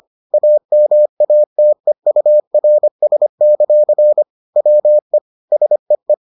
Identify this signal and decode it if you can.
Morse code, heard as a keyed tone, about 25 wpm, 600 Hz. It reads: AMATEURS; WE SEE